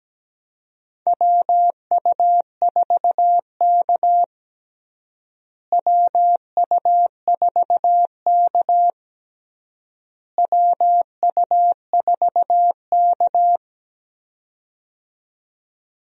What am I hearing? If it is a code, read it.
Morse code, heard as a keyed tone, about 17 wpm, 700 Hz. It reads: WU4K WU4K WU4K
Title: USA Call Signs - 4 Characters - Repeated 3x - 17wpm